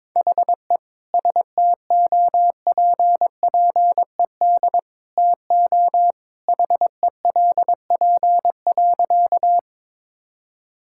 Morse code, heard as a keyed tone, about 22 wpm, 700 Hz. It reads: HE STOPPED TO HELP.
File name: practice / 22wpm / Sentences with 2-Word Phrases - 22wpm